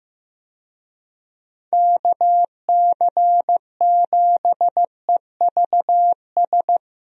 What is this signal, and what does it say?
Morse code, heard as a keyed tone, about 15 wpm, 700 Hz. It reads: KC7EVS
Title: USA Call Signs - 6 Characters - 15wpm